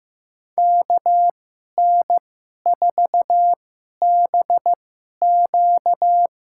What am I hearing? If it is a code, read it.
Morse code, heard as a keyed tone, about 15 wpm, spaced as for 11 wpm, 700 Hz. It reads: KN4BQ
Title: USA Call Signs - 2x Character Spacing - 15wpm